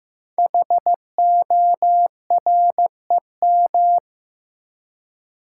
Morse code, heard as a keyed tone, about 15 wpm, 700 Hz. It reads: HOREM
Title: Code Groups - 5 Letters - 15wpm